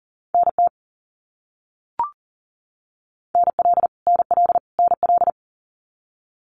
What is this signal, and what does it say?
Morse code, heard as a keyed tone, about 40 wpm, 700 Hz. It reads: NT NL NL NL
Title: Canadian Provinces and Territories - Repeated 3x - 40wpm